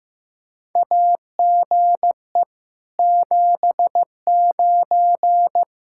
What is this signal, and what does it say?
Morse code, heard as a keyed tone, about 15 wpm, 700 Hz. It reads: AGE 79
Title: Practice QSO - Exchange 3 - More Info - 15wpm